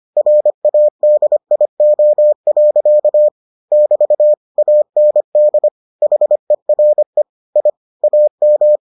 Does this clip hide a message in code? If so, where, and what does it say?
Yes: Morse code from 0.2 to 8.8 s, heard as a keyed tone, about 25 wpm, 600 Hz. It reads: RADIO. <BT> AND HERE I AM